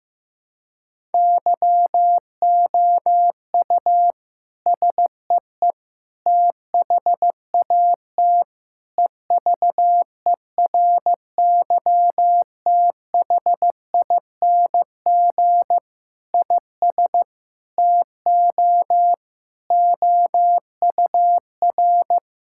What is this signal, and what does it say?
Morse code, heard as a keyed tone, about 15 wpm, 700 Hz. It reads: YOU SEE THAT EVERYTHING IS TO OUR
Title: Sentences from Top 1000 Words - Review Part 6 - 15wpm